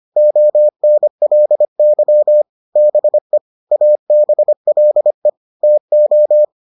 Morse code, heard as a keyed tone, about 25 wpm, 600 Hz. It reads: ONLY BE ABLE TO